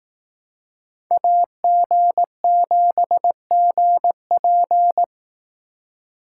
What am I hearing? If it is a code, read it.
Morse code, heard as a keyed tone, about 18 wpm, 700 Hz. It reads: AG7GP
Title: Speed Racing - USA Call Signs 15wpm